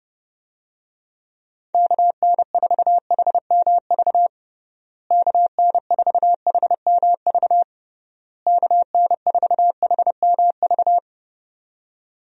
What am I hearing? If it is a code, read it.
Morse code, heard as a keyed tone, about 29 wpm, 700 Hz. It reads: KN4HMV KN4HMV KN4HMV
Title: USA Call Signs - 6 Characters - Repeated 3x - 30wpm